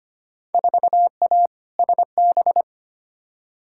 Morse code, heard as a keyed tone, about 25 wpm, 700 Hz. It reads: 4A SB
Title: Field Day Exchanges Encore - 25wpm